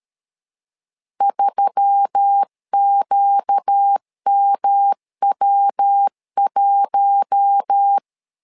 Morse code, heard as a keyed tone, about 13 wpm, 800 Hz. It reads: 3QMW1
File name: LEZIONI / 4 Q1S3 12wpm 2023